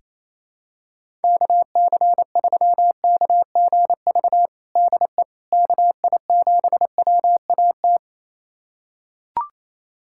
Morse code, heard as a keyed tone, about 28 wpm, 700 Hz. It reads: KC3KGV DE KI7WAT
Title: USA Callsign DE Callsign 28wpm